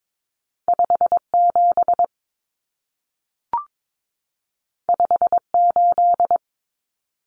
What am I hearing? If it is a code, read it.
Morse code, heard as a keyed tone, about 22 wpm, 700 Hz. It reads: 57 58
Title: Numbers - 2-Digit 22wpm